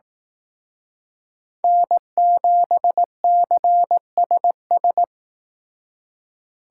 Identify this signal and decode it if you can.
Morse code, heard as a keyed tone, about 19 wpm, 700 Hz. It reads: N7CSS
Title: Speed Racing - USA 1x3 Call Signs 15wpm